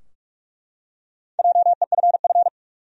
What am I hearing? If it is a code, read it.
Morse code, heard as a keyed tone, about 42 wpm, 700 Hz. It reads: JEFF
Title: USA Common Names - Max 4 Letters - 45wpm